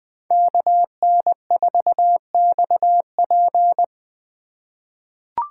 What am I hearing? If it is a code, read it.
Morse code, heard as a keyed tone, about 20 wpm, 700 Hz. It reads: KN4XP
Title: 2023 POTA Hunter Call Signs - 20wpm